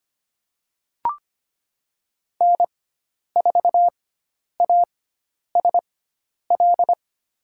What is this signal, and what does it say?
Morse code, heard as a keyed tone, about 25 wpm, spaced as for 10 wpm, 700 Hz. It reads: N4ASL
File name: Lesson 16g - Letter c - Call Signs 25wpm